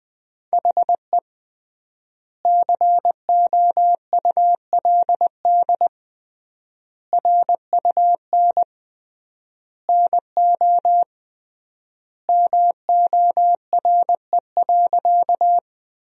Morse code, heard as a keyed tone, about 21 wpm, 700 Hz. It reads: HE COULD RUN NO MORE.
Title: 3x Word-Spacing - Sentences from Top 300 Words - 20wpm